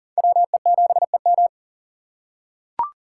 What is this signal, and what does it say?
Morse code, heard as a keyed tone, about 39 wpm, 700 Hz. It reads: WE7EM
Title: USA Call Signs - 5 Characters - 3x Character Spacing - 40wpm